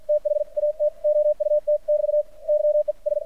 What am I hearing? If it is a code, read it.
Morse code, heard as a keyed tone, about 53 wpm, spaced as for 44 wpm, 600 Hz. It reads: THAT OUT <BT> YES
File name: archived / CRSnet